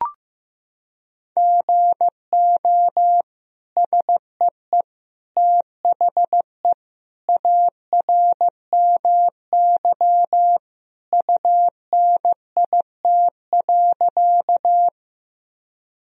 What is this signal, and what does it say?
Morse code, heard as a keyed tone, about 15 wpm, 700 Hz. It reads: GO SEE THE ARMY UNIT.
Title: Sentences from Top 1000 Words - Review Part 2 - 15wpm